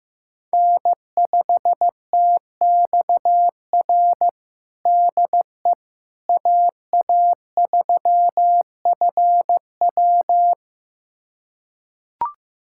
Morse code, heard as a keyed tone, about 15 wpm, 700 Hz. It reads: N5TXR DE AA3FW
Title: USA Callsign DE Callsign - 5 Characters - 15wpm